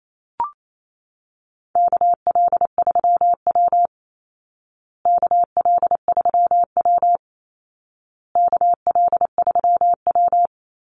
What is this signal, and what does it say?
Morse code, heard as a keyed tone, about 27 wpm, 700 Hz. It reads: KL3W KL3W KL3W
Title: Call Signs Repeated 2x - 28wpm